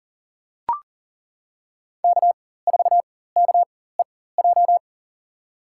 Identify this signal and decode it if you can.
Morse code, heard as a keyed tone, about 38 wpm, spaced as for 18 wpm, 700 Hz. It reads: K4KEJ
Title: USA Call Signs - 4x Character Spacing - 40wpm